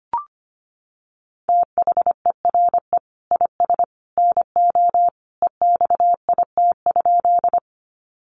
Morse code, heard as a keyed tone, about 25 wpm, 690 Hz. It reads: THERE IS NO EXIT?
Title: Sentences from Top 1100 Words 25wpm